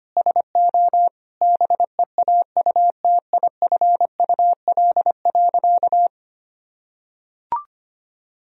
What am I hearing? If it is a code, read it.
Morse code, heard as a keyed tone, about 25 wpm, 700 Hz. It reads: SO BEAUTIFUL.
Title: Sentences from Top 1000 Words - Review Part 3 - 25wpm